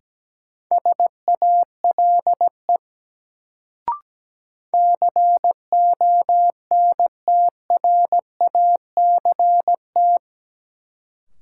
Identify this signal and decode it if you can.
Morse code, heard as a keyed tone, about 17 wpm, 700 Hz. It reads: SALE CONTRACT
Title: Sentences from Top 1400 Words - New Word Drill Alt - 17wpm